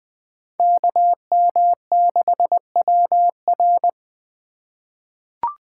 Morse code, heard as a keyed tone, about 21 wpm, 700 Hz. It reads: KM6WR
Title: USA Call Signs - 3x Character Spacing - 20wpm